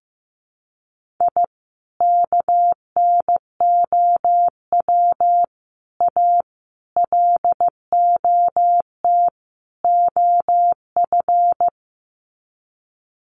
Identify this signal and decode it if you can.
Morse code, heard as a keyed tone, about 15 wpm, 700 Hz. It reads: I KNOW A LOT OF
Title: Sets of 5 Words Encore 15wpm